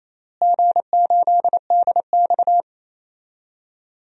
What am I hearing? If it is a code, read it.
Morse code, heard as a keyed tone, about 28 wpm, 700 Hz. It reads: G8DX
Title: DXCC-100 Call Signs - 4 Characters - Repeated 2x - 28wpm